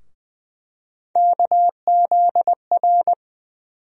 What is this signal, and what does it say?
Morse code, heard as a keyed tone, about 20 wpm, 700 Hz. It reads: KZR
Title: USA Call Sign Postfixes - 3 Letters 20wpm